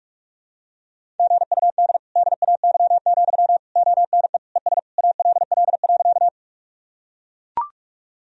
Morse code, heard as a keyed tone, about 42 wpm, 700 Hz. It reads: GUD DAY, ONE ES ALL.
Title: Sentences from Top 200 Words and Abbreviations 45wpm